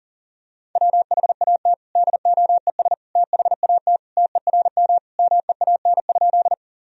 Morse code, heard as a keyed tone, about 40 wpm, 700 Hz. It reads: WHAT DOES THAT TERM MEAN?
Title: Sentences from Top 1000 Words - Review Part 4 - 40wpm